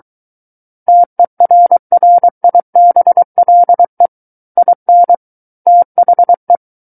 Morse code, heard as a keyed tone, about 23 wpm, 700 Hz. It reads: TERRIBLE IN THE
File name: Book One / 14-Chapter 14 - 23 WPM